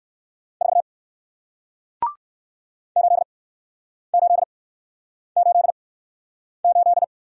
Morse code, heard as a keyed tone, about 58 wpm, 700 Hz. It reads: 4 8 8 8 8